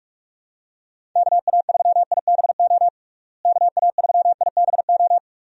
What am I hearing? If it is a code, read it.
Morse code, heard as a keyed tone, about 46 wpm, 700 Hz. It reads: KA3IBO KA3IBO
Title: USA Call Signs - 6 Characters - Repeated 2x - 45wpm